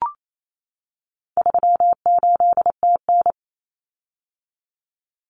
Morse code, heard as a keyed tone, about 28 wpm, 700 Hz. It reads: 38TN
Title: Common Cut Numbers - 4-Digit 28wpm